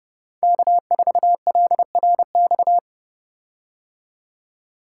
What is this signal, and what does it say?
Morse code, heard as a keyed tone, about 30 wpm, 700 Hz. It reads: K4LRX
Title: K1USN SST QSO Practice - 30wpm